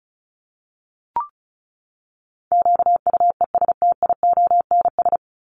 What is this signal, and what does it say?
Morse code, heard as a keyed tone, about 34 wpm, 700 Hz. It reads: QUESTIONS
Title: ES:S Suffix - Top 100 Words with Base Words 35wpm